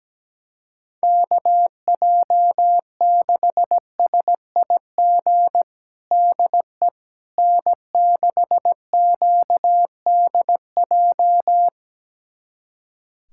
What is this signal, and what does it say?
Morse code, heard as a keyed tone, about 17 wpm, 700 Hz. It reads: KJ6SIG DE N6QDJ
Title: USA Callsign DE Callsign 17wpm